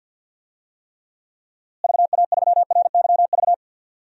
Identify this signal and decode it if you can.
Morse code, heard as a keyed tone, about 46 wpm, 700 Hz. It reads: VA3RYV
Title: DXCC-100 Call Signs - 6 Characters - Repeated 2x - 50wpm